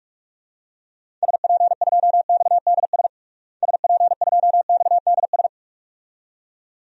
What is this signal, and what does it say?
Morse code, heard as a keyed tone, about 46 wpm, 700 Hz. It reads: SP2XDS SP2XDS
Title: DXCC-100 Call Signs - 6 Characters - Repeated 2x - 45wpm